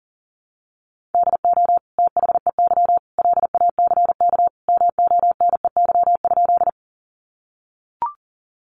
Morse code, heard as a keyed tone, about 40 wpm, 700 Hz. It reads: DO THEY LACK MONEY?
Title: Sentences from Top 1100 Words 40wpm